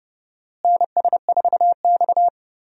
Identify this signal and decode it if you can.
Morse code, heard as a keyed tone, about 29 wpm, 700 Hz. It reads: NS4X
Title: USA Call Signs - 4 Characters - Repeated 3x - 30wpm